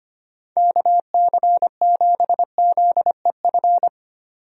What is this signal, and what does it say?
Morse code, heard as a keyed tone, about 25 wpm, 700 Hz. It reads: KC7ZEF
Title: USA Call Signs - 6 Characters - 25wpm